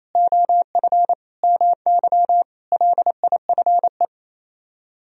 Morse code, heard as a keyed tone, about 34 wpm, 700 Hz. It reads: OF MY LIFE
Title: Speed-Racing - Sets of 5 Words 28wpm